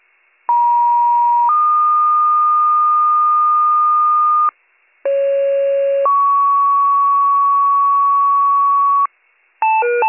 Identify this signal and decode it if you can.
Two-tone page, heard as a one-second tone, then a three-second tone, about 950 then 1220 Hz, about 550 then 1120 Hz. It is heard from 0.5 to 9.1 s.